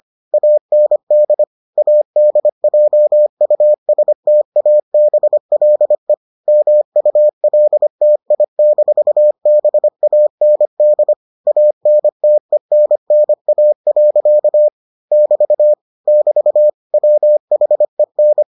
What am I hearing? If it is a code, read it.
Morse code, heard as a keyed tone, about 25 wpm, 600 Hz. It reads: AND ADJUSTABLE MULTI-BAND ANTENNA. <BT> <BT> WHEN